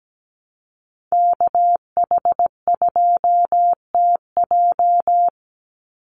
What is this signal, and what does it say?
Morse code, heard as a keyed tone, about 17 wpm, 700 Hz. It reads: KH2TJ
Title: NA SOTA Chaser Callsigns 17wpm